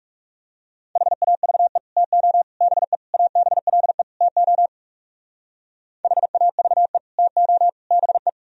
Morse code, heard as a keyed tone, about 37 wpm, 700 Hz. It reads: HAVE TO BE ABLE TO HAVE TO BE
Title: Speed-Racing - Sets of 5 Words 30wpm